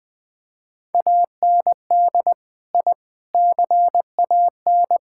Morse code, heard as a keyed tone, about 20 wpm, 700 Hz. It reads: AND I CAN
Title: Sentences with 4-Word Phrases - Spoken Beginning - 20wpm